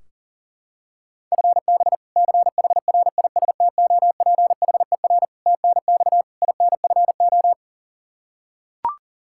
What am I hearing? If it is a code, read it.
Morse code, heard as a keyed tone, about 40 wpm, 700 Hz. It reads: FB CHRISTOPHER TNX INFO
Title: K1USN SST QSO Practice 40wpm